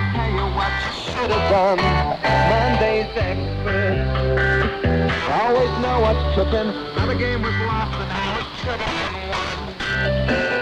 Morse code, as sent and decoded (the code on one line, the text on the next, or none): none